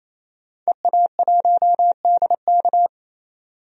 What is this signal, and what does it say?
Morse code, heard as a keyed tone, about 28 wpm, 700 Hz. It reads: EA1DK
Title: DXCC-100 Call Signs - 5 Characters - 28wpm